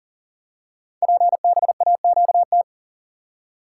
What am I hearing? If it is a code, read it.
Morse code, heard as a keyed tone, about 42 wpm, 700 Hz. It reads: PBAQT
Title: Code Groups - 5 Letters - 40wpm